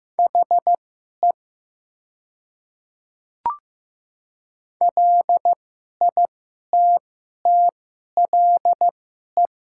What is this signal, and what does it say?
Morse code, heard as a keyed tone, about 15 wpm, spaced as for 11 wpm, 700 Hz. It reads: HE LITTLE
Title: 2x Character Spacing - Top 200 Words - 15wpm